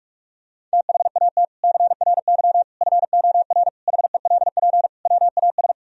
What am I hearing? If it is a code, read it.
Morse code, heard as a keyed tone, about 46 wpm, 700 Hz. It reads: THAT CRY FOR HELP WAS
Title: Sentences from Top 1000 Words - Review Part 3 - 45wpm